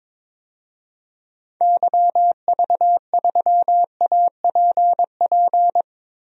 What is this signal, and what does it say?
Morse code, heard as a keyed tone, about 22 wpm, 700 Hz. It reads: YV3APP
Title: DXCC-100 Call Signs - 6 Characters - Repeated 2x - 22wpm